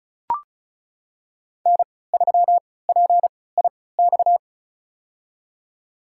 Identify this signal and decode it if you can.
Morse code, heard as a keyed tone, about 34 wpm, spaced as for 20 wpm, 700 Hz. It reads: N3PIX
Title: USA Call Signs - 5 Characters - 3x Character Spacing - 35wpm